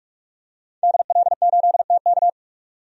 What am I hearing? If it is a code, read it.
Morse code, heard as a keyed tone, about 46 wpm, 700 Hz. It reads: DL8TK
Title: DXCC-100 Call Signs - 5 Characters - 45wpm